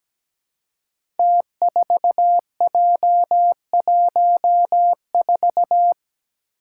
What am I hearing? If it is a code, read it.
Morse code, heard as a keyed tone, about 17 wpm, 700 Hz. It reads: T4J14
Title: Five Challenging Letter-Number Combinations 17wpm